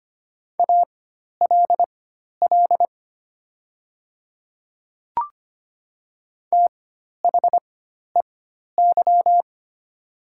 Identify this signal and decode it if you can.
Morse code, heard as a keyed tone, about 25 wpm, spaced as for 12 wpm, 700 Hz. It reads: ALL THEY
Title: 4x Character Spacing - Top 100 Words - 25wpm